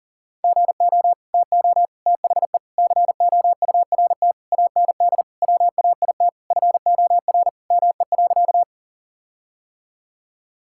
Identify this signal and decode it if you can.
Morse code, heard as a keyed tone, about 40 wpm, 700 Hz. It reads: GO TO THE COURT AND WAIT FOR ME.
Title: Sentences from Top 1600 Words 40wpm